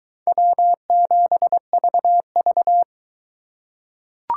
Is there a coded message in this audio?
Yes: Morse code from 0.3 to 2.8 s, heard as a keyed tone, about 22 wpm, 700 Hz. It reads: W7VV